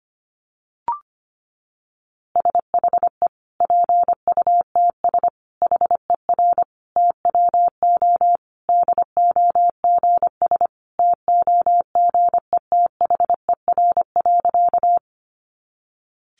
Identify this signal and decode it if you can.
Morse code, heard as a keyed tone, about 25 wpm, 700 Hz. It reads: SHE PUTS HER TWO DOGS TOGETHER.
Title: Easy Phrases 25wpm